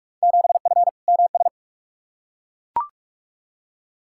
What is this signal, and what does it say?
Morse code, heard as a keyed tone, about 41 wpm, 700 Hz. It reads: 7F MS
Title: Field Day Exchanges Encore - 45wpm